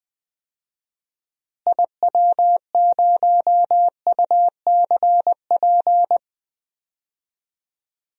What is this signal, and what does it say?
Morse code, heard as a keyed tone, about 20 wpm, 700 Hz. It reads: IW0UCP
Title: DXCC-100 Call Signs - 6 Characters - Repeated 3x - 20wpm